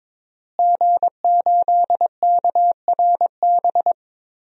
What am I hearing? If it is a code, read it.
Morse code, heard as a keyed tone, about 22 wpm, 700 Hz. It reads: G8KRB
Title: DXCC-100 Call Signs - 5 Characters - Repeated 2x - 22wpm